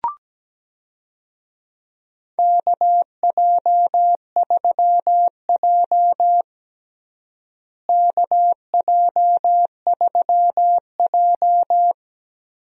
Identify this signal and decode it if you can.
Morse code, heard as a keyed tone, about 17 wpm, 700 Hz. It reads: KJ3J KJ3J
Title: USA Call Signs - 4 Characters - Repeated 3x - 17wpm